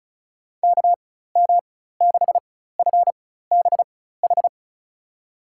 Morse code, heard as a keyed tone, about 36 wpm, spaced as for 16 wpm, 700 Hz. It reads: KM6FBH
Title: USA Call Signs - 6 Characters - 4x Character Spacing - 35wpm